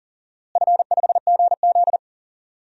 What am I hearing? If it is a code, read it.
Morse code, heard as a keyed tone, about 40 wpm, 700 Hz. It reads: F5GZ
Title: DXCC-100 Call Signs - 4 Characters - Repeated 2x - 40wpm